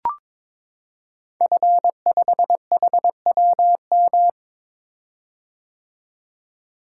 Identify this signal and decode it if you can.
Morse code, heard as a keyed tone, about 22 wpm, 700 Hz. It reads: F5HWM
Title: Code Groups - 5 Letter:Numbers - 22wpm